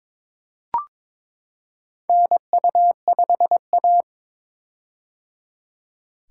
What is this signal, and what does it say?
Morse code, heard as a keyed tone, about 22 wpm, 700 Hz. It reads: NU5A
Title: K1USN SST QSO Practice - 22wpm